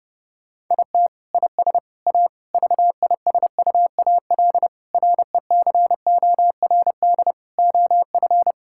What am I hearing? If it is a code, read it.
Morse code, heard as a keyed tone, about 30 wpm, 700 Hz. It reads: IT IS A VISUAL RECORD OF
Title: Sentences from Top 1000 Words - Review Part 6 - 30wpm